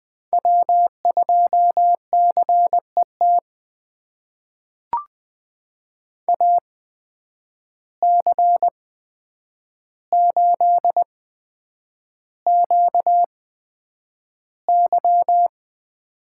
Morse code, heard as a keyed tone, about 20 wpm, spaced as for 9 wpm, 700 Hz. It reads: W2CET AC8QY
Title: USA Call Signs - 8x Character Spacing - 20wpm